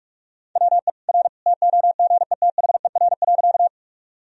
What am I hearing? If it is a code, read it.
Morse code, heard as a keyed tone, about 41 wpm, 700 Hz. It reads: WE R TOGETHER.
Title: Sentences from Top 300 Words and Abbreviations 45wpm